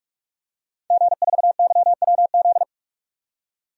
Morse code, heard as a keyed tone, about 46 wpm, 700 Hz. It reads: G4YWZ
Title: DXCC-100 Call Signs - 5 Characters - 45wpm